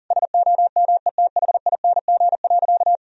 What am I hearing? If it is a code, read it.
Morse code, heard as a keyed tone, about 40 wpm, 700 Hz. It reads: SOMETHING.
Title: Sentences from Top 1000 Words - Review Part 3 - 40wpm